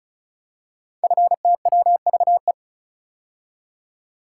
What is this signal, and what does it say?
Morse code, heard as a keyed tone, about 36 wpm, 700 Hz. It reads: FTWVE
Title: Code Groups - 5 Letters - 35wpm